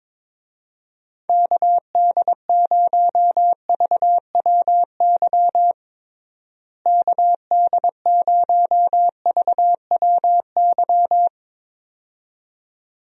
Morse code, heard as a keyed tone, about 22 wpm, 700 Hz. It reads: KD0VWY KD0VWY
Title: USA Call Signs - 6 Characters - Repeated 2x - 22wpm